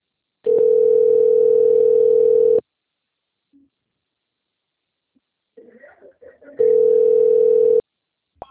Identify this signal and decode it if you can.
Ringback tone (North America), 440 + 480 Hz, with speech.
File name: rg-1600-unknown- - - .63523